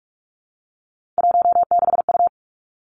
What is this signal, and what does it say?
Morse code, heard as a keyed tone, about 46 wpm, 700 Hz. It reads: 16U
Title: Three Challenging Letter-Number Combinations 45wpm